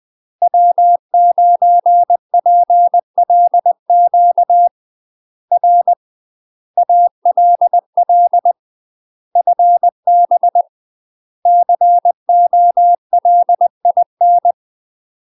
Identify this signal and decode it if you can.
Morse code, heard as a keyed tone, about 20 wpm, 700 Hz. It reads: W9PLQ R ALL FB COLIN
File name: UK MASTER QSOs TRAINING / 20WPM / UK MASTER QSOs 0024